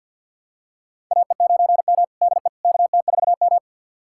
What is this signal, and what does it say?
Morse code, heard as a keyed tone, about 52 wpm, 700 Hz. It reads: AE9M DE KT4M
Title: USA Callsign DE Callsign - 4 Characters - 50wpm